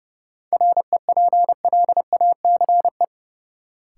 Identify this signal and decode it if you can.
Morse code, heard as a keyed tone, about 32 wpm, 700 Hz. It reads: REPLACE
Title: Sentences from Top 1400 Words - New Word Drill - 30wpm